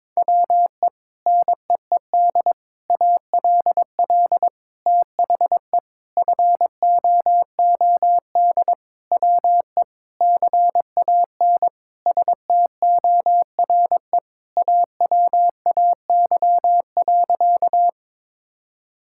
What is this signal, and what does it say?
Morse code, heard as a keyed tone, about 22 wpm, 700 Hz. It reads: WE NEED ALL THE FOOD WE CAN STORE AWAY.
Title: Sentences from Top 1000 Words - Review Part 5 - 22wpm